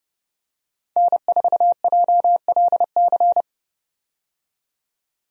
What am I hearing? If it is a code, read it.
Morse code, heard as a keyed tone, about 32 wpm, 700 Hz. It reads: N4JLC
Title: USA Call Signs - 5 Characters - 30wpm